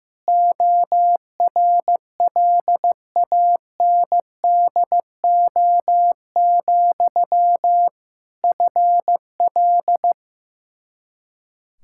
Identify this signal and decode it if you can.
Morse code, heard as a keyed tone, about 15 wpm, 700 Hz. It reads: ORLANDO, FL
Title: Practice QSO - Exchange 1 - Beginning - 15wpm